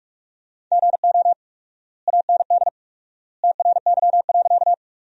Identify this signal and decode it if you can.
Morse code, heard as a keyed tone, about 46 wpm, 700 Hz. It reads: GO AND TRY.
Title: 4x Word-Spacing - Sentences from Top 200 Words - 45wpm